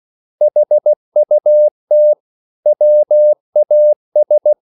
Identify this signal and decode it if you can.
Morse code, heard as a keyed tone, about 16 wpm, 600 Hz. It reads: HUT WAS